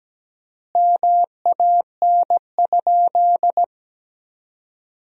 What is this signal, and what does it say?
Morse code, heard as a keyed tone, about 17 wpm, 700 Hz. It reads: MAN?
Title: 4x Word-Spacing - Sentences from Top 400 Words - 17wpm